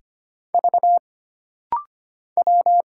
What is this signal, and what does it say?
Morse code, heard as a keyed tone, about 25 wpm, 700 Hz. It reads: V W
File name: Single Character (Letter:Number:Punctuation:Prosign) - Rapid-Fire - 25wpm